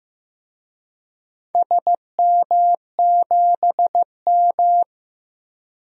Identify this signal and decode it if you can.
Morse code, heard as a keyed tone, about 15 wpm, 700 Hz. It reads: SM7M
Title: DXCC-100 Call Signs - 4 Characters - 15wpm